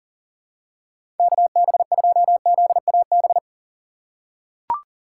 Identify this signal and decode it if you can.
Morse code, heard as a keyed tone, about 40 wpm, 700 Hz. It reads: KB2ZAB
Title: USA Call Signs - 2x Character Spacing - 40wpm